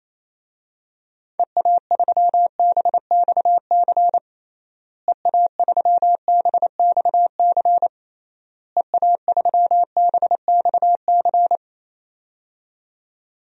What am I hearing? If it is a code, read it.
Morse code, heard as a keyed tone, about 28 wpm, 700 Hz. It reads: EA3BXC EA3BXC EA3BXC
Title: DXCC-100 Call Signs - 6 Characters - Repeated 3x - 28wpm